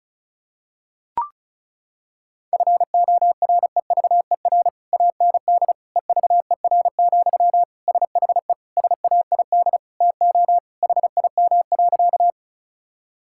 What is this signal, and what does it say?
Morse code, heard as a keyed tone, about 34 wpm, 700 Hz. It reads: FOREVER AND EVER, SHE SAID TO HIM.
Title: Sentences from Top 1600 Words 35wpm